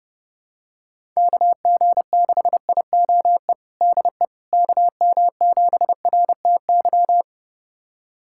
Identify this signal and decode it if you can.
Morse code, heard as a keyed tone, about 30 wpm, 700 Hz. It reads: KG6IOE DE KM7RTY
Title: USA Callsign DE Callsign 30wpm